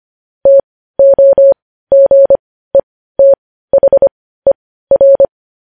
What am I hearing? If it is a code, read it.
Morse code, heard as a keyed tone, about 25 wpm, spaced as for 15 wpm, 550 Hz. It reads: TOGETHER
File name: podcast / News headlines-1x-25x15